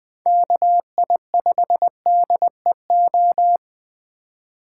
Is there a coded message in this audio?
Yes: Morse code from 0.3 to 3.6 s, heard as a keyed tone, about 20 wpm, 700 Hz. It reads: KI5DEO